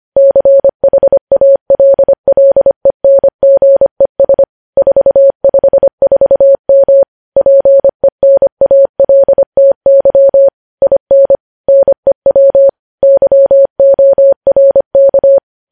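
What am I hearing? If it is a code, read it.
Morse code, heard as a keyed tone, about 25 wpm, 550 Hz. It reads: CHALLENGES 454M PENALTY IN NEW YORK
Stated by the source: Morse code news headlines at 25, opening with vvv.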